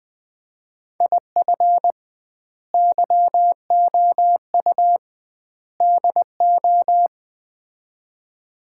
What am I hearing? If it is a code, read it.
Morse code, heard as a keyed tone, about 20 wpm, 700 Hz. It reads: IF YOU DO
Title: 2x Word-Spacing - Sets of 3 Words - 20wpm